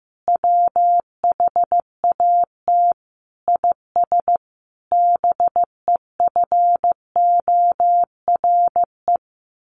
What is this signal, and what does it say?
Morse code, heard as a keyed tone, about 15 wpm, 700 Hz. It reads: WHAT IS BEFORE
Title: Sentences from Top 600 Words 15wpm